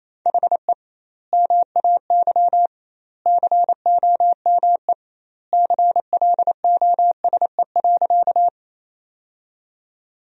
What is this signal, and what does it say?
Morse code, heard as a keyed tone, about 28 wpm, 700 Hz. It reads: HE MAY COME CLOSE.
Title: 2x Word-Spacing - Sentences from Top 300 Words - 28wpm